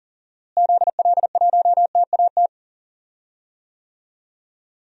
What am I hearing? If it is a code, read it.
Morse code, heard as a keyed tone, about 37 wpm, 700 Hz. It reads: ZL1TAT
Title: DXCC-100 Call Signs - 6 Characters - Repeated 3x - 40wpm